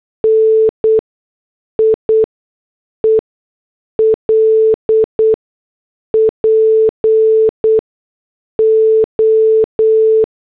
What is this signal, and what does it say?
Morse code, heard as a keyed tone, about 8 wpm, spaced as for 6 wpm, 440 Hz. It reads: NIEL PO